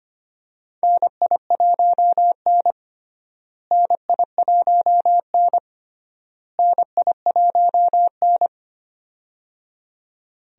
Morse code, heard as a keyed tone, about 25 wpm, 700 Hz. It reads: NI1N NI1N NI1N
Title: USA Call Signs - 4 Characters - Repeated 3x - 25wpm